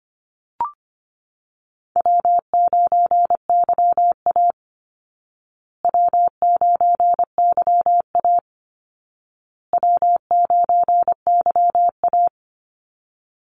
Morse code, heard as a keyed tone, about 25 wpm, 700 Hz. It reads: W9YA W9YA W9YA
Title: Call Signs Repeated 2x - 25wpm